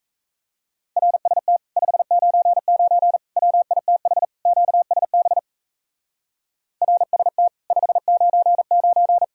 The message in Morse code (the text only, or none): RST 599 WITH QSB RST 599